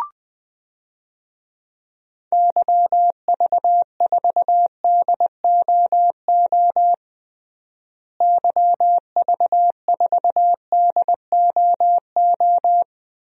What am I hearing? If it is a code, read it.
Morse code, heard as a keyed tone, about 21 wpm, 700 Hz. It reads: YV4DOO YV4DOO
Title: DXCC-100 Call Signs - 6 Characters - Repeated 3x - 20wpm